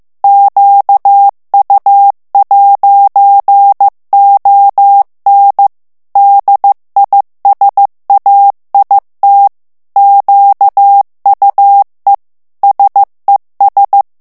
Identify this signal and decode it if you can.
Morse code, heard as a keyed tone, about 15 wpm, 800 Hz. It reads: QU'ON DISAIT QUE SES
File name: morse mp3 / FRANCAIS TEXTE CLAIR 2 VITESSE 14 MOTS MINUTES